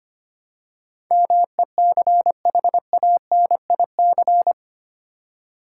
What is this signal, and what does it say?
Morse code, heard as a keyed tone, about 25 wpm, 700 Hz. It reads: MECHANIC
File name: Eight Letter Words - 25wpm